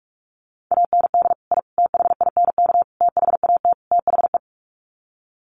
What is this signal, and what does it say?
Morse code, heard as a keyed tone, about 44 wpm, 700 Hz. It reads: AND I THINK THAT THE
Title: Sets of 5 Words 45wpm